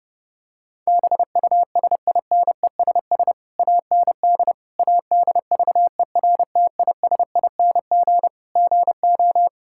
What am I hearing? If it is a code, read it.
Morse code, heard as a keyed tone, about 32 wpm, 700 Hz. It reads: BUSINESS AND ADVERTISING GO
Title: Sentences from Top 1000 Words - Review Part 6 - 30wpm